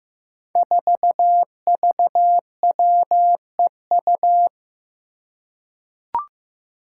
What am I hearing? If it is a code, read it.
Morse code, heard as a keyed tone, about 15 wpm, 700 Hz. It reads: 4VWEU
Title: Five Challenging Letter-Number Combinations 15wpm